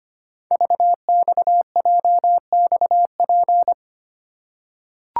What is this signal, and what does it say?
Morse code, heard as a keyed tone, about 25 wpm, 700 Hz. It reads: VXJXP
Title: Code Groups - 5 Letter:Numbers - 25wpm